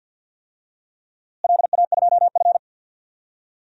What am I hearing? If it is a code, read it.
Morse code, heard as a keyed tone, about 47 wpm, 700 Hz. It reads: LA2F
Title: DXCC-100 Call Signs - 4 Characters - Repeated 2x - 50wpm